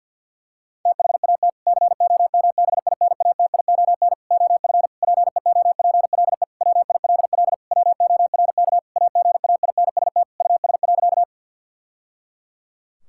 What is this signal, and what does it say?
Morse code, heard as a keyed tone, about 50 wpm, 700 Hz. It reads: THAT COMBINATION OF PEOPLE WILL WORK AGAINST US.
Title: Sentences from Top 1000 Words - Review Part 6 - 50wpm